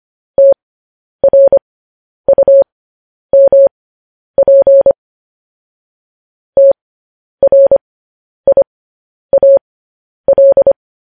Morse code, heard as a keyed tone, about 25 wpm, spaced as for 10 wpm, 550 Hz. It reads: TRUMP TRIAL
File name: podcast / News headlines-1x-25x10